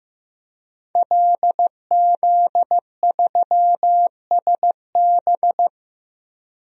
Morse code, heard as a keyed tone, about 15 wpm, 700 Hz. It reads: LZ3SB
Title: DXCC-100 Call Signs - 5 Characters - Repeated 3x - 15wpm